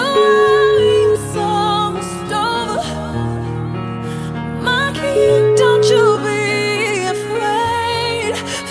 Ringback tone (Europe, 425 Hz), with noise.